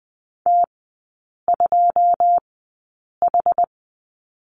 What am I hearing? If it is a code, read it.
Morse code, heard as a keyed tone, about 20 wpm, spaced as for 8 wpm, 700 Hz. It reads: T2H
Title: 093 Lesson 16e - Letter c - Three Characters 20wpm